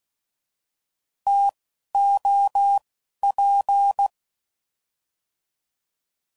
Morse code, heard as a keyed tone, about 16 wpm, 800 Hz. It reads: T O P